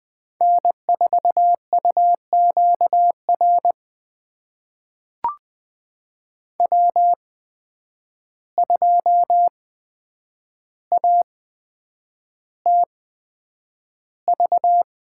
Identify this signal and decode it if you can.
Morse code, heard as a keyed tone, about 20 wpm, spaced as for 9 wpm, 700 Hz. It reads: N4UQR W2ATV
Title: USA Call Signs - 5 Characters - 8x Character Spacing - 20wpm